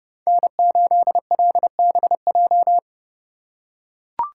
Morse code, heard as a keyed tone, about 29 wpm, 700 Hz. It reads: N8LBJ